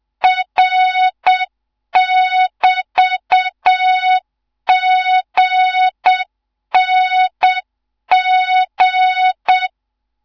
Morse code, heard as a keyed tone, about 7 wpm, 3700 Hz, with noise. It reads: R<BT>GNG
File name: Morze-HA2MN / hang / 15 GYAKORLAT